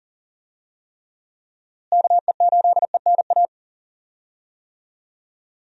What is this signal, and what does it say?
Morse code, heard as a keyed tone, about 40 wpm, 700 Hz. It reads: KE8ENA
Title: USA Call Signs - 6 Characters - Repeated 3x - 40wpm